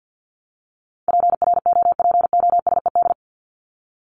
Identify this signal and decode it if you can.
Morse code, heard as a keyed tone, about 50 wpm, 700 Hz. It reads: PROPOSED
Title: ED Suffix - Top 100 Words with Base Words 50wpm